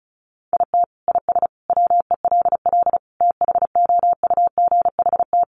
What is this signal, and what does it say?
Morse code, heard as a keyed tone, about 34 wpm, 700 Hz. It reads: IT IS WELL THOUGHT
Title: Sentences from Top 300 Words 35wpm